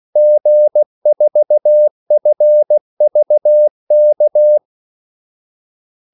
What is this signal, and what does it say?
Morse code, heard as a keyed tone, about 16 wpm, 600 Hz. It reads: G4FVK
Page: Kaldesignaler 16 wpm